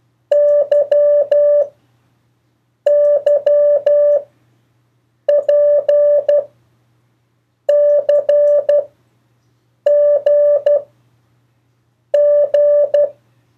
Morse code, heard as a keyed tone, about 12 wpm, 580 Hz. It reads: Y Y P C G G
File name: 1of4-cw-letters